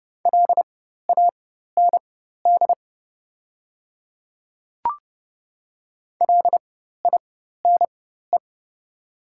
Morse code, heard as a keyed tone, about 31 wpm, spaced as for 14 wpm, 690 Hz. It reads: LAND LINE
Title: 4x Character Spacing - Top 200 Words - 30wpm